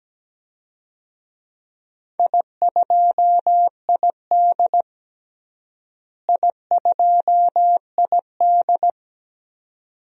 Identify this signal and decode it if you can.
Morse code, heard as a keyed tone, about 17 wpm, 700 Hz. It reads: I2ID I2ID
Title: DXCC-100 Call Signs - 4 Characters - Repeated 2x - 17wpm